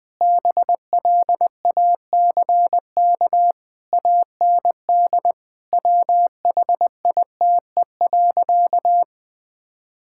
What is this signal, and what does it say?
Morse code, heard as a keyed tone, about 21 wpm, 700 Hz. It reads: BLACK AND WHITE.
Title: Sentences with 5-Word Phrases - 20wpm